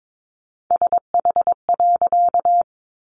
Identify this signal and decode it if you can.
Morse code, heard as a keyed tone, about 22 wpm, 700 Hz. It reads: SH.
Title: A Sound Of Thunder 22wpm